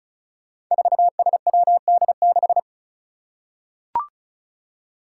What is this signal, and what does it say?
Morse code, heard as a keyed tone, about 36 wpm, 700 Hz. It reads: 4SWD6
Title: Five Challenging Letter-Number Combinations 35wpm